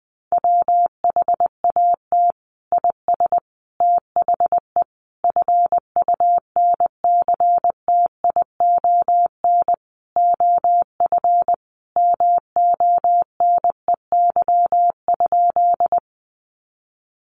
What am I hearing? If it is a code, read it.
Morse code, heard as a keyed tone, about 20 wpm, 700 Hz. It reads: WHAT IS THE FUNCTION OF MONEY?
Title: Sentences from Top 1100 Words 20wpm